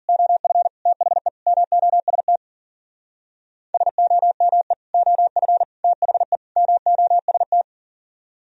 Morse code, heard as a keyed tone, about 50 wpm, 700 Hz. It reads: OF THE MOST SOME OF THE MOST